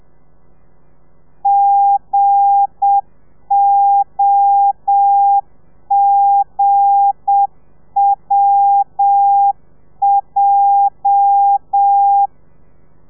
Morse code, heard as a keyed tone, about 7 wpm, 800 Hz. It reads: GOGWJ
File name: telegrafia / i2tfj / Lez8 30